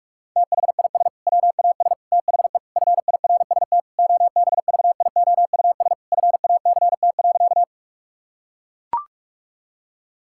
Morse code, heard as a keyed tone, about 41 wpm, 700 Hz. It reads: THIS WAS THE FIRST OBVIOUS FACT.
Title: Sentences from Top 1600 Words 45wpm